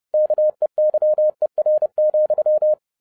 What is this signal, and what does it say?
Morse code, heard as a keyed tone, about 30 wpm, 610 Hz. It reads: KEYER,
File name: 2012-01-mp3 / n11-I DIDNT MISS THE SOLAR CYCLE 24-30wpm